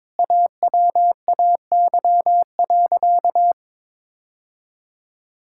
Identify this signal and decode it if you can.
Morse code, heard as a keyed tone, about 22 wpm, 700 Hz. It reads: AWAY.
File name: practice / 22wpm / Sentences with 2-Word Phrases - 22wpm